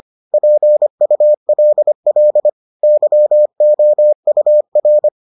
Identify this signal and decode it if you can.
Morse code, heard as a keyed tone, about 25 wpm, 600 Hz. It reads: PULL YOUR